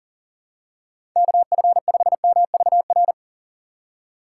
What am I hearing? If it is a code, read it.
Morse code, heard as a keyed tone, about 40 wpm, 700 Hz. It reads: KF5MVR
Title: USA Call Signs - 6 Characters - 2x Character Spacing - 40wpm